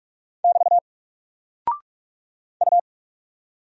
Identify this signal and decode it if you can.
Morse code, heard as a keyed tone, about 47 wpm, 700 Hz. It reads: <BT> U
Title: Single Character (Letter:Number:Punctuation:Prosign) Alt 45wpm